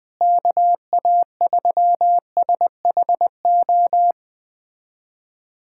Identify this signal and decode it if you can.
Morse code, heard as a keyed tone, about 20 wpm, 700 Hz. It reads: KA3SHO
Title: USA Callsign DE Callsign 20wpm